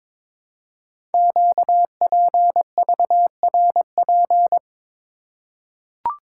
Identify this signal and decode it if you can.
Morse code, heard as a keyed tone, about 22 wpm, 700 Hz. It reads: QPVRP
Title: Code Groups - 5 Letters - 22wpm